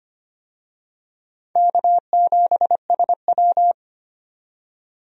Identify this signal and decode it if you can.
Morse code, heard as a keyed tone, about 25 wpm, 700 Hz. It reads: K7SW
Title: USA Call Signs - 4 Characters - 25wpm